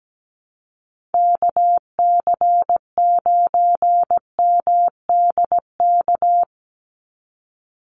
Morse code, heard as a keyed tone, about 17 wpm, 700 Hz. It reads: KC9MDK
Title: USA Call Signs 17wpm